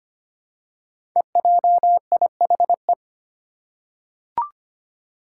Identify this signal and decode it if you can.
Morse code, heard as a keyed tone, about 25 wpm, 700 Hz. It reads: EJIHE
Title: Five Challenging Letter-Number Combinations 25wpm